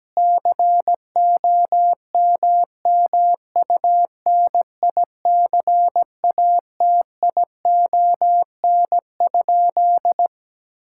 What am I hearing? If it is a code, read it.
Morse code, heard as a keyed tone, about 17 wpm, 700 Hz. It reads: COMMUNICATION?
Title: Sentences from Top 1000 Words - Review Part 6 - 17wpm